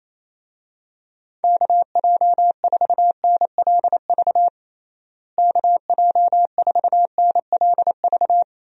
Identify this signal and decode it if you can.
Morse code, heard as a keyed tone, about 28 wpm, 700 Hz. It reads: KJ4NLV KJ4NLV
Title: USA Call Signs - 6 Characters - Repeated 2x - 28wpm